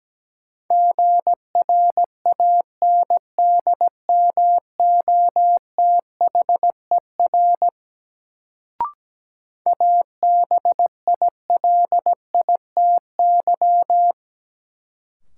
Morse code, heard as a keyed tone, about 17 wpm, 700 Hz. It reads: GRANDMOTHER ABILITY
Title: Sentences from Top 1200 Words - New Word Drill Alt - 17wpm